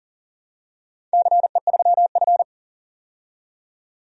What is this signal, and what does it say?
Morse code, heard as a keyed tone, about 39 wpm, 700 Hz. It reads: CE3F
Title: DXCC-100 Call Signs - 4 Characters - 40wpm